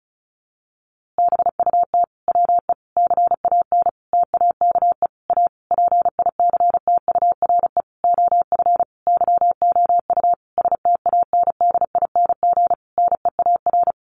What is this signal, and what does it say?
Morse code, heard as a keyed tone, about 34 wpm, 700 Hz. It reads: BUT WE CAN TAKE A PICTURE OF YOU STANDING NEAR
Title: A Sound Of Thunder 35wpm